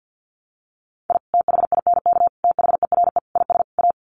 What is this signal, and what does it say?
Morse code, heard as a keyed tone, about 50 wpm, 700 Hz. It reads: I THINK THERE IS A
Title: Sets of 5 Words 50wpm